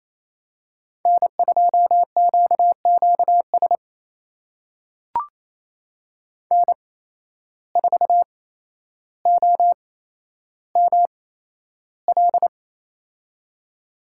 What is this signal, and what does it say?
Morse code, heard as a keyed tone, about 28 wpm, spaced as for 12 wpm, 700 Hz. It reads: N2QQS N4OML
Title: USA Call Signs - 5 Characters - 8x Character Spacing - 28wpm